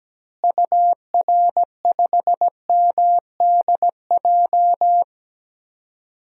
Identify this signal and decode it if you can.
Morse code, heard as a keyed tone, about 17 wpm, 700 Hz. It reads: UR5MDJ
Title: DXCC-100 Call Signs - 6 Characters - Repeated 3x - 17wpm